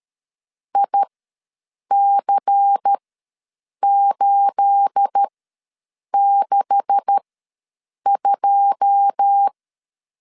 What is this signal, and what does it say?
Morse code, heard as a keyed tone, about 13 wpm, spaced as for 7 wpm, 800 Hz. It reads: IC862
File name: LEZIONI / 9 286 ws 12wpm 3351